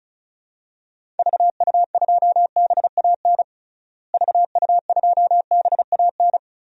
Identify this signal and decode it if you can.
Morse code, heard as a keyed tone, about 36 wpm, 700 Hz. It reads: VU2BAN VU2BAN
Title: DXCC-100 Call Signs - 6 Characters - Repeated 3x - 35wpm